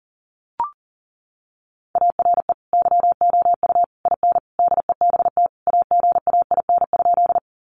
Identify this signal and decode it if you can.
Morse code, heard as a keyed tone, about 40 wpm, 700 Hz. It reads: ARE YOU IN DEBT AGAIN?
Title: Sentences from Top 1300 Words 40wpm